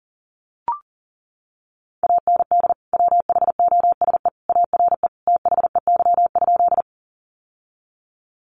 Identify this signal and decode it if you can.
Morse code, heard as a keyed tone, about 40 wpm, 700 Hz. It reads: AND WHOSE ARE THEY?
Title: Sentences from Top 1200 Words 40wpm